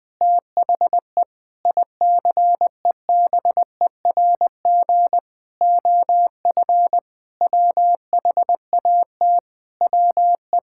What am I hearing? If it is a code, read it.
Morse code, heard as a keyed tone, about 20 wpm, 700 Hz. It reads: THE ICEBERG OF WHAT WE
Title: Sentences with 8-Word Phrases - 20wpm